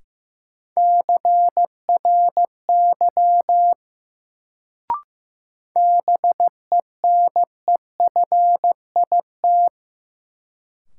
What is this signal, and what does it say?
Morse code, heard as a keyed tone, about 15 wpm, 700 Hz. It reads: CRY BENEFIT
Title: Sets of 5 Words Encore - New Word Drill - 15wpm